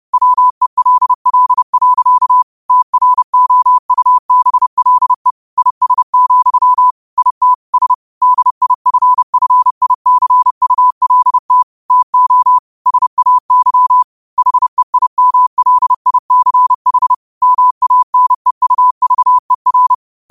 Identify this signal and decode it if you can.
Morse code, heard as a keyed tone, about 30 wpm, 1000 Hz. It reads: WELL. TROUBLE IS, IT S DIFFICULT TO SAY HEIMLICH MANEUVER
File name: qotdfiles / 30wpm